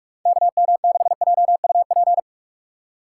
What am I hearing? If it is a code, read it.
Morse code, heard as a keyed tone, about 43 wpm, 700 Hz. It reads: KM6JUP
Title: USA Call Signs - 6 Characters - 2x Character Spacing - 45wpm